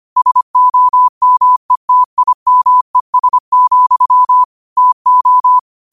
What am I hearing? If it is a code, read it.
Morse code, heard as a keyed tone, about 25 wpm, 1000 Hz. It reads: SOMETIMES, TO